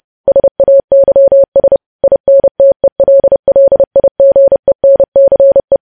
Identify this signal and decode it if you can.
Morse code, heard as a keyed tone, about 30 wpm, 550 Hz. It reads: SAYS INTELLIGENCE
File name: podcast / News headlines-1x-30x30